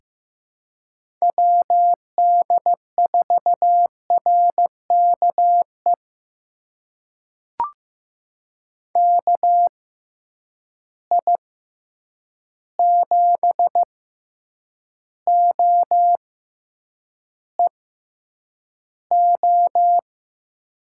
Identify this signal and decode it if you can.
Morse code, heard as a keyed tone, about 15 wpm, spaced as for 8 wpm, 700 Hz. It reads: WD4RKE KI7OEO
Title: USA Call Signs - 6 Characters - 6x Character Spacing - 15wpm